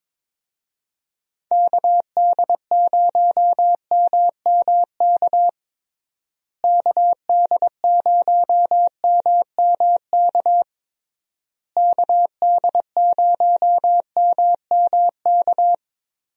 Morse code, heard as a keyed tone, about 22 wpm, 700 Hz. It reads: KD0MMK KD0MMK KD0MMK
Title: USA Call Signs - 6 Characters - Repeated 3x - 22wpm